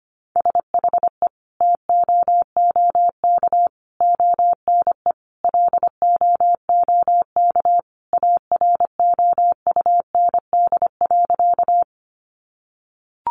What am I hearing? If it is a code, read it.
Morse code, heard as a keyed tone, about 25 wpm, 700 Hz. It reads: SHE TOOK ONE LOOK AROUND.
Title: Sentences from Top 600 Words 25wpm